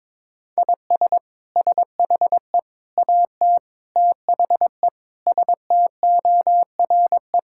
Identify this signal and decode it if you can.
Morse code, heard as a keyed tone, about 22 wpm, 700 Hz. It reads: IS SHE AT THE STORE
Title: Sentences from Top 1000 Words - Review Part 5 - 22wpm